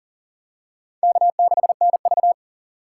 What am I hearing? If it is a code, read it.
Morse code, heard as a keyed tone, about 40 wpm, 700 Hz. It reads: K6NV
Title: USA Call Signs - 4 Characters - 2x Character Spacing - 40wpm